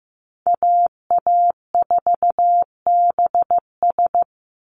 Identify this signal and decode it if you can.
Morse code, heard as a keyed tone, about 15 wpm, 700 Hz. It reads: AA4BS
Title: USA Call Signs 15wpm